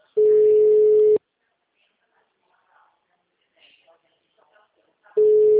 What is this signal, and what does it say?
Ringback tone (Europe, 425 Hz), with speech.
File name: out- -2011- - - .18772